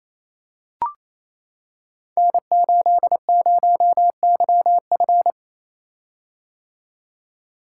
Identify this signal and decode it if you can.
Morse code, heard as a keyed tone, about 27 wpm, 700 Hz. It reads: N80YF
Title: Code Groups - 5 Letter:Numbers - 28wpm